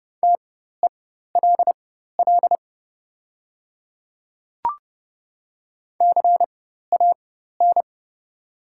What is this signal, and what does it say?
Morse code, heard as a keyed tone, about 30 wpm, spaced as for 14 wpm, 700 Hz. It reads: TELL CAN
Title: 4x Character Spacing - Top 500 Words - 30wpm